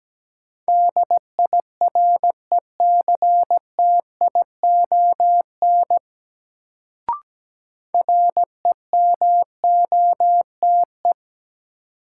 Morse code, heard as a keyed tone, about 17 wpm, 700 Hz. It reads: DIRECTION REMOTE
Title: Sentences from Top 1300 Words - New Word Drill Alt - 17wpm